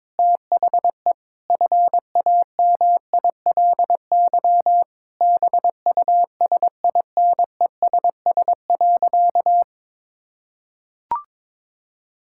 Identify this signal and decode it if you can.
Morse code, heard as a keyed tone, about 22 wpm, 700 Hz. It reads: THE FAMILY BUSINESS.
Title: Sentences from Top 1000 Words - Review Part 6 - 22wpm